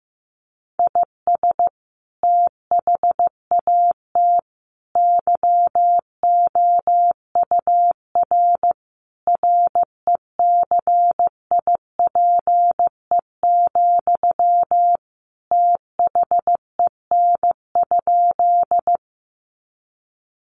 Morse code, heard as a keyed tone, about 15 wpm, 700 Hz. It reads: IS THAT YOUR RECIPE, THEN?
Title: Sentences from Top 1200 Words 15wpm